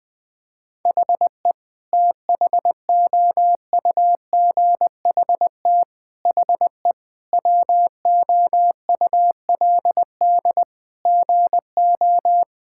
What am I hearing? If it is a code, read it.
Morse code, heard as a keyed tone, about 20 wpm, 700 Hz. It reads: HE THOUGHT HE WOULD GO
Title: Sentences from Top 1000 Words - Review Part 6 - 20wpm